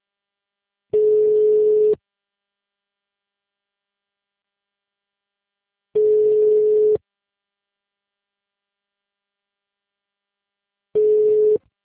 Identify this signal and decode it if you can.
Ringback tone (Europe, 425 Hz).